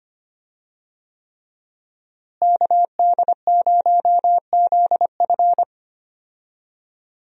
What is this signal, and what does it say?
Morse code, heard as a keyed tone, about 25 wpm, 700 Hz. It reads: KD0ZF
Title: USA Call Signs - 5 Characters - Repeated 3x - 25wpm